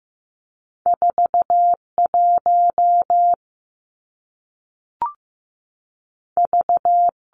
Morse code, heard as a keyed tone, about 15 wpm, 700 Hz. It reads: 4 1 V
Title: Two Challenging Letter-Number Combinations 15wpm